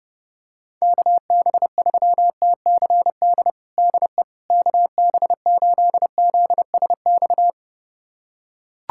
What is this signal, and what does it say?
Morse code, heard as a keyed tone, about 32 wpm, 700 Hz. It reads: KB3TCD DE KB8ZSX
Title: USA Callsign DE Callsign - 6 Characters - 30wpm